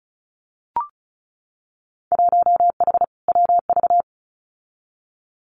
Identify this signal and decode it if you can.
Morse code, heard as a keyed tone, about 36 wpm, 700 Hz. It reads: 1H WV
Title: Winter Field Day Exchanges Encore 35wpm